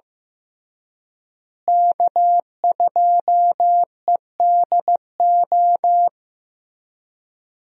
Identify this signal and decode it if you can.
Morse code, heard as a keyed tone, about 15 wpm, 700 Hz. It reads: K2EDO
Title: USA Call Signs - 5 Characters - Repeated 3x - 15wpm